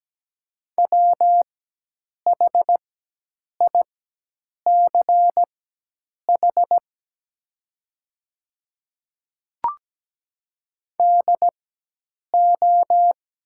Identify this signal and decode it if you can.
Morse code, heard as a keyed tone, about 17 wpm, spaced as for 8 wpm, 700 Hz. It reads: WHICH DO
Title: 4x Character Spacing - Top 100 Words - 17wpm